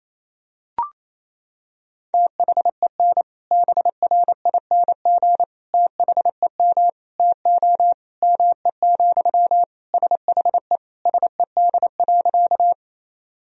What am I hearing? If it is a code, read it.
Morse code, heard as a keyed tone, about 28 wpm, 700 Hz. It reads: THEN BRING THEM TO ME, SHE SED.
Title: Sentences from Top 500 Words and Abbreviations 28wpm